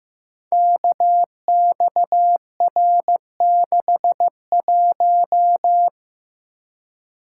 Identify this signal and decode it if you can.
Morse code, heard as a keyed tone, about 15 wpm, 700 Hz. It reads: KXR61
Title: Code Groups - 5 Letter:Numbers - 15wpm